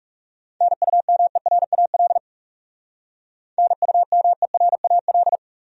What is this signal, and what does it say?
Morse code, heard as a keyed tone, about 36 wpm, 700 Hz. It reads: NUMERAL NUMERAL
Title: Speed-Racing Top 500 Words 30wpm